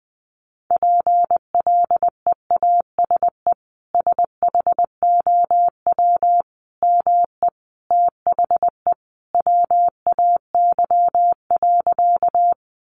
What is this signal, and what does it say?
Morse code, heard as a keyed tone, about 20 wpm, 700 Hz. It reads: PLEASE SHOW ME THE WAY.
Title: Sentences from Top 1500 Words 20wpm